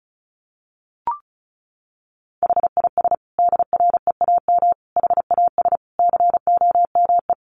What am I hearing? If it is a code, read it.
Morse code, heard as a keyed tone, about 34 wpm, 700 Hz. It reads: HIS DREAM HAS COME
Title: Sentences from Top 1500 Words 35wpm